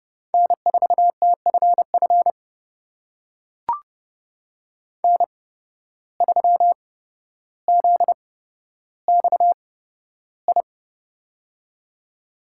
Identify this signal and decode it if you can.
Morse code, heard as a keyed tone, about 30 wpm, spaced as for 13 wpm, 700 Hz. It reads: N4TFF N3ZXI
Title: USA Call Signs - 5 Characters - 8x Character Spacing - 30wpm